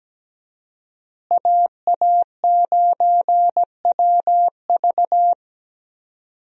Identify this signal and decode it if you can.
Morse code, heard as a keyed tone, about 17 wpm, 700 Hz. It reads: AA9WV
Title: USA Call Signs - 5 Characters - Repeated 3x - 17wpm